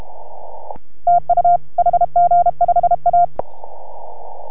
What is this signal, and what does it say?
Morse code, heard as a keyed tone, about 32 wpm, 690 Hz. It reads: TU HG5A